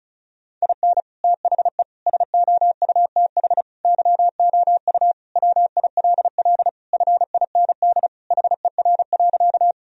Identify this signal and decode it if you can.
Morse code, heard as a keyed tone, about 36 wpm, 700 Hz. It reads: IN THE SOUTH YOU WILL FIND HER.
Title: Sentences from Top 1000 Words - Review Part 2 - 35wpm